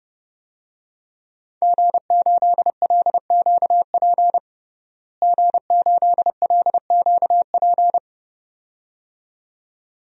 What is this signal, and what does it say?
Morse code, heard as a keyed tone, about 30 wpm, 700 Hz. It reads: G8LQP G8LQP
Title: DXCC-100 Call Signs - 5 Characters - Repeated 2x - 30wpm